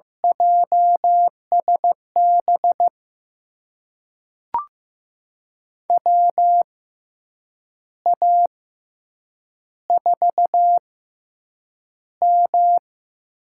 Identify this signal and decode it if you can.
Morse code, heard as a keyed tone, about 15 wpm, spaced as for 7 wpm, 700 Hz. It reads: JSB WA4M
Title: USA Call Signs - 6 Characters - 6x Character Spacing - 15wpm